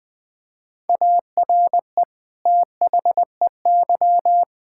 Morse code, heard as a keyed tone, about 20 wpm, 700 Hz. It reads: ARE THEY
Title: Sentences from Top 1000 Words - Review Part 5 - 20wpm